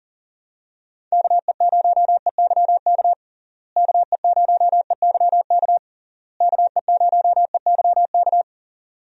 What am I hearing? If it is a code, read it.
Morse code, heard as a keyed tone, about 40 wpm, 700 Hz. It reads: KE0EYK KE0EYK KE0EYK
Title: USA Call Signs - 6 Characters - Repeated 3x - 40wpm